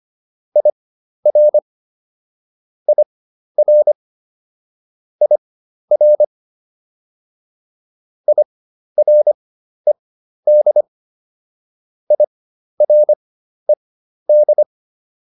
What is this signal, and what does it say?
Morse code, heard as a keyed tone, about 25 wpm, spaced as for 12 wpm, 600 Hz. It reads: IR IR IR IRED IRED
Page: fundamental-practice-files